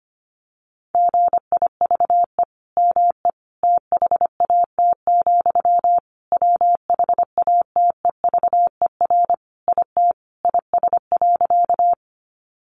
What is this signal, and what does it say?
Morse code, heard as a keyed tone, about 25 wpm, 700 Hz. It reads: GIVE ME THAT, WHATEVER IT IS.
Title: Sentences from Top 1100 Words 25wpm